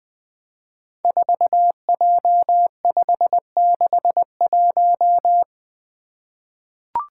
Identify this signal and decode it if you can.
Morse code, heard as a keyed tone, about 20 wpm, 700 Hz. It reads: 4J561
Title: Five Challenging Letter-Number Combinations 20wpm